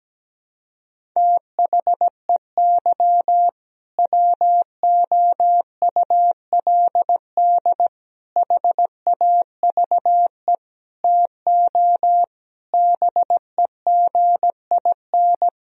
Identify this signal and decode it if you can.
Morse code, heard as a keyed tone, about 17 wpm, 700 Hz. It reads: THEY WOULD HAVE TO BEGIN
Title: Sentences from Top 1000 Words - Review Part 1 - 17wpm